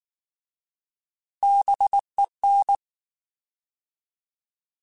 Morse code, heard as a keyed tone, about 19 wpm, 800 Hz. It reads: BEN
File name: Alphabetical / Alphabetical mp3s / Bs @18